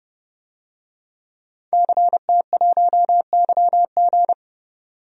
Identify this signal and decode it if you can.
Morse code, heard as a keyed tone, about 30 wpm, 700 Hz. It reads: CT1YG
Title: DXCC-100 Call Signs - 5 Characters - 30wpm